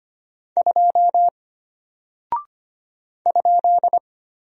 Morse code, heard as a keyed tone, about 25 wpm, 700 Hz. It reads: 2 ?
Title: Single Character (Letter:Number:Punctuation:Prosign) Alt 25wpm